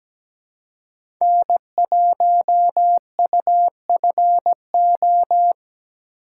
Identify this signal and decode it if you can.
Morse code, heard as a keyed tone, about 17 wpm, 700 Hz. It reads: N1UFO
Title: USA Call Signs - 5 Characters - 4x Character Spacing - 17wpm